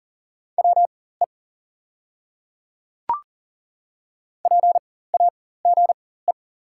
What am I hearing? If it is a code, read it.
Morse code, heard as a keyed tone, about 40 wpm, spaced as for 19 wpm, 690 Hz. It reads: WE PAGE
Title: 4x Character Spacing - Top 500 Words - 40wpm